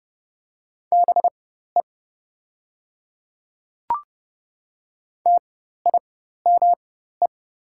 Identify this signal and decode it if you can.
Morse code, heard as a keyed tone, about 30 wpm, spaced as for 14 wpm, 690 Hz. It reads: BE TIME
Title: 4x Character Spacing - Top 100 Words - 30wpm